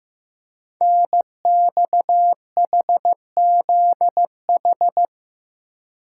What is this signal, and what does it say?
Morse code, heard as a keyed tone, about 15 wpm, 700 Hz. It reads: NXHZH
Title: Code Groups - 5 Letters - 15wpm